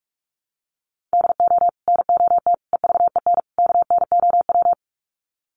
Morse code, heard as a keyed tone, about 44 wpm, 700 Hz. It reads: DO NOT EVEN KNOW
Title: Sets of 4 Words 45wpm